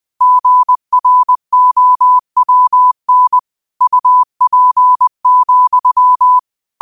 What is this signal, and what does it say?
Morse code, heard as a keyed tone, about 20 wpm, 1000 Hz. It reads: GROWN UP,
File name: qotdfiles / 20wpm